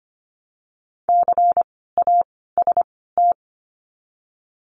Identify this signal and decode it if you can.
Morse code, heard as a keyed tone, about 25 wpm, spaced as for 16 wpm, 700 Hz. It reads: CAST
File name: course / 25:10wpm / 094 Lesson 16f - Letter c - Words 25wpm